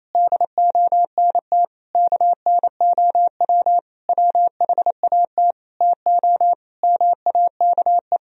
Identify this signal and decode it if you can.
Morse code, heard as a keyed tone, about 28 wpm, 700 Hz. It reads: DONT KNOW WHAT TO MAKE
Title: Sentences from Top 1000 Words - Review Part 1 - 28wpm